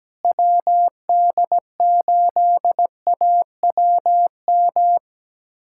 Morse code, heard as a keyed tone, about 17 wpm, 700 Hz. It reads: WD8AWM
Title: USA Callsign DE Callsign 17wpm